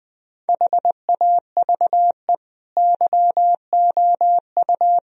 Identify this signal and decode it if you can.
Morse code, heard as a keyed tone, about 20 wpm, 700 Hz. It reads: HAVE YOU
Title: Sentences from Top 1000 Words - Review Part 3 - 20wpm